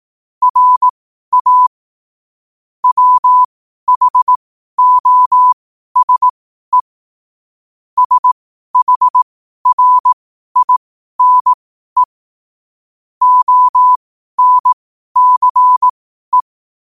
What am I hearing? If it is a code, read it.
Morse code, heard as a keyed tone, about 18 wpm, spaced as for 12 wpm, 1000 Hz. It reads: RA WHOSE SHRINE ONCE
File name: qotdfiles / 12wpm